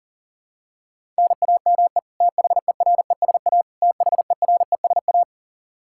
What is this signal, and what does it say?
Morse code, heard as a keyed tone, about 40 wpm, 700 Hz. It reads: NAME THERESA THERESA
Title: Practice QSO - Exchange 1 - Beginning - 40wpm